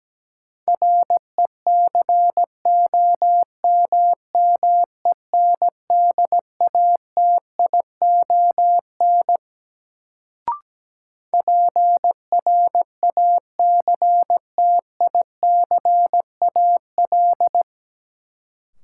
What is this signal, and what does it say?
Morse code, heard as a keyed tone, about 17 wpm, 700 Hz. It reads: RECOMMENDATION PRACTICAL
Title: Sentences from Top 1400 Words - New Word Drill Alt - 17wpm